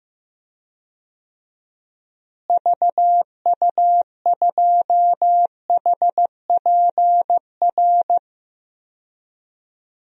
Morse code, heard as a keyed tone, about 15 wpm, 700 Hz. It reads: VU2HPR
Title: DXCC-100 Call Signs - 6 Characters - Repeated 2x - 15wpm